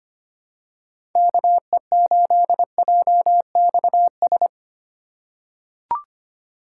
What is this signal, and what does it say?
Morse code, heard as a keyed tone, about 25 wpm, 700 Hz. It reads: KE8JXS
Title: USA Call Signs - 6 Characters - 4x Character Spacing - 25wpm